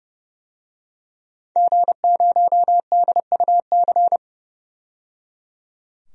Morse code, heard as a keyed tone, about 30 wpm, 700 Hz. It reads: G0DUC
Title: DXCC-100 Call Signs - 5 Characters - 30wpm